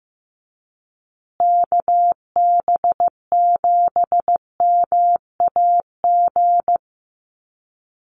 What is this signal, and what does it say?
Morse code, heard as a keyed tone, about 15 wpm, 700 Hz. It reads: KB7MAG
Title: USA Call Signs Repeated 3x 15wpm